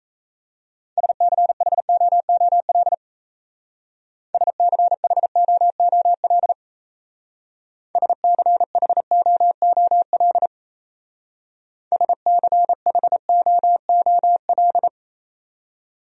Morse code, heard as a keyed tone, about 37 wpm, 700 Hz. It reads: SCHOOL SCHOOL SCHOOL SCHOOL